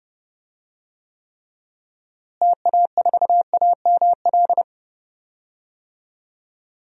Morse code, heard as a keyed tone, about 32 wpm, 700 Hz. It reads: TA4AML
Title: DXCC-100 Call Signs - 6 Characters - Repeated 3x - 30wpm